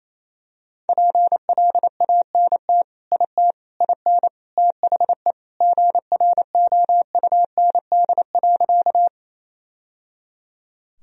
Morse code, heard as a keyed tone, about 28 wpm, 700 Hz. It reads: PLANT IT IN THE GROUND.
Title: Sentences from Top 1000 Words - Review Part 1 - 28wpm